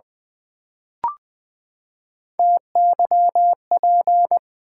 Morse code, heard as a keyed tone, about 20 wpm, 700 Hz. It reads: TYP
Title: Build A Word - 3 Letters At A Time - 9-Letter Words - 20wpm